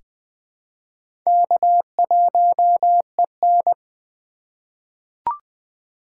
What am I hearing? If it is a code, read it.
Morse code, heard as a keyed tone, about 20 wpm, 700 Hz. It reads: K1EN
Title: USA Call Signs - 4 Characters - 4x Character Spacing - 20wpm